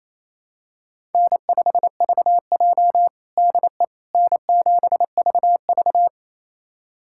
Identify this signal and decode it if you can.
Morse code, heard as a keyed tone, about 28 wpm, 700 Hz. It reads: N5VJ DE N7VV
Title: USA Callsign DE Callsign - 4 Characters - 28wpm